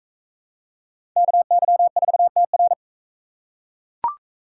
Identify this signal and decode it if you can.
Morse code, heard as a keyed tone, about 40 wpm, 700 Hz. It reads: KY4TR